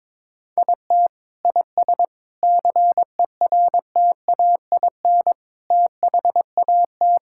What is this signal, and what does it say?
Morse code, heard as a keyed tone, about 22 wpm, 700 Hz. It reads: IT IS CERTAIN THAT
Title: Sentences from Top 1000 Words - Review Part 5 - 22wpm